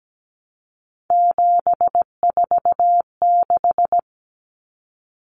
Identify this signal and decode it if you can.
Morse code, heard as a keyed tone, about 17 wpm, 690 Hz. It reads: 746
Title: Common Cut Numbers - 3-Digit 17wpm